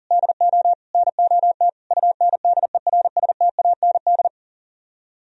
Morse code, heard as a keyed tone, about 51 wpm, 700 Hz. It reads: DO NOT UNDERSTAND